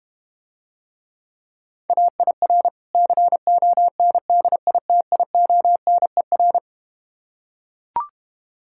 Morse code, heard as a keyed tone, about 32 wpm, 700 Hz. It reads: AIR CONDITIONER